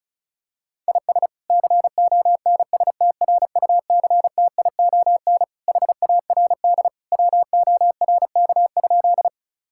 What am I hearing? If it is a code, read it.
Morse code, heard as a keyed tone, about 36 wpm, 700 Hz. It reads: IS CONSTRUCTION HARD WORK?
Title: Sentences from Top 1600 Words 35wpm